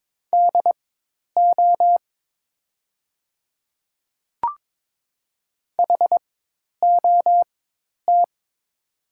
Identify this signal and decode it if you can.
Morse code, heard as a keyed tone, about 22 wpm, spaced as for 10 wpm, 700 Hz. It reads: DO HOT
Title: 4x Character Spacing - Top 100 Words - 22wpm